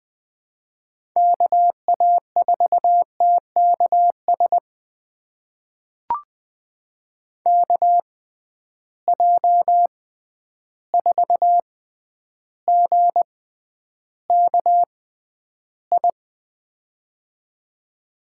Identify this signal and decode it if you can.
Morse code, heard as a keyed tone, about 21 wpm, spaced as for 10 wpm, 700 Hz. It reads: KA4TKS KJ4GKI
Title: USA Call Signs - 6 Characters - 6x Character Spacing - 20wpm